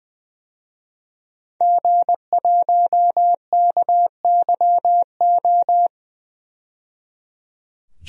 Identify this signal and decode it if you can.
Morse code, heard as a keyed tone, about 20 wpm, 700 Hz. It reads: G1KYO
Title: DXCC-100 Call Signs - 5 Characters - 20wpm